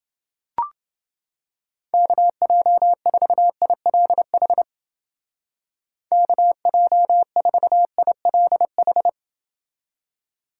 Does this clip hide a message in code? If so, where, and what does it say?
Yes: Morse code from 1.9 to 9.1 s, heard as a keyed tone, about 29 wpm, 700 Hz. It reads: KJ4ILH KJ4ILH